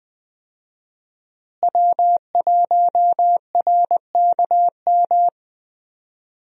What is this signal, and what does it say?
Morse code, heard as a keyed tone, about 21 wpm, 700 Hz. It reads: W1RKM
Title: USA Call Signs - 5 Characters - Repeated 2x - 20wpm